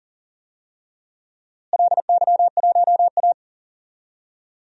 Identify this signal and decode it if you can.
Morse code, heard as a keyed tone, about 40 wpm, 700 Hz. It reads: LY1A
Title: DXCC-100 Call Signs - 4 Characters - Repeated 3x - 40wpm